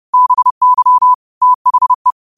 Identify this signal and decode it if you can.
Morse code, heard as a keyed tone, about 30 wpm, 1000 Hz. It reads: BY THE